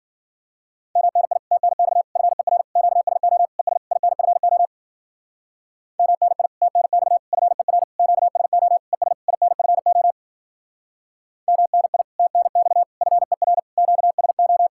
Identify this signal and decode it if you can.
Morse code, heard as a keyed tone, about 51 wpm, 700 Hz. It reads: MNI TNX FER QSO ES INFO MNI TNX FER QSO ES INFO MNI TNX FER QSO